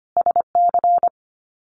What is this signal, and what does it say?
Morse code, heard as a keyed tone, about 24 wpm, 700 Hz. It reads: SC
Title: US States Repeated 3x 25wpm